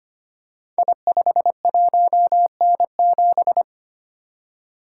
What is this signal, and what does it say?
Morse code, heard as a keyed tone, about 25 wpm, 700 Hz. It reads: I51N7
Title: Code Groups - 5 Letter:Numbers - 25wpm